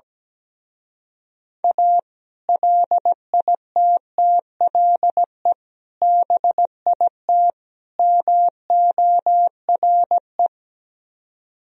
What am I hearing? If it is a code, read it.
Morse code, heard as a keyed tone, about 17 wpm, 700 Hz. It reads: A LITTLE BIT MORE
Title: Speed-Racing - Sets of 4 Words 17wpm